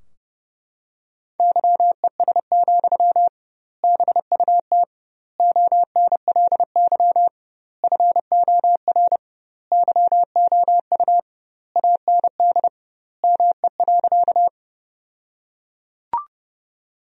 Morse code, heard as a keyed tone, about 32 wpm, 700 Hz. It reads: YES, BUT ONLY FOR YOU AND ME.
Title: 2x Word-Spacing - Sentences from Top 500 Words - 30wpm